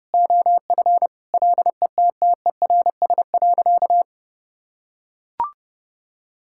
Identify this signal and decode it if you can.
Morse code, heard as a keyed tone, about 32 wpm, 700 Hz. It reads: OF LETTERS.
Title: Sentences from Top 1000 Words - Review Part 6 - 30wpm